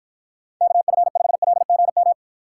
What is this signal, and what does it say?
Morse code, heard as a keyed tone, about 53 wpm, 700 Hz. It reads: KF5LGM